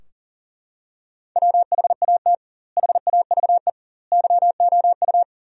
Morse code, heard as a keyed tone, about 37 wpm, 700 Hz. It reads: WHAT HAVE YOU
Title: 2x Word-Spacing - Sentences from Top 100 Words - 40wpm